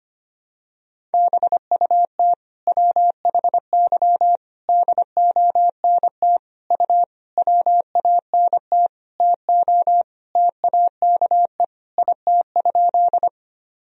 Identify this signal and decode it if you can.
Morse code, heard as a keyed tone, about 25 wpm, 700 Hz. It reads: BUT WHY DONT U WANT TO TAKE IT?
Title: Sentences from Top 300 Words and Abbreviations 25wpm